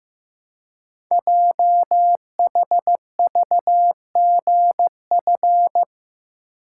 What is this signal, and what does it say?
Morse code, heard as a keyed tone, about 15 wpm, 700 Hz. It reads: JHVGF
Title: Code Groups - 5 Letters - 15wpm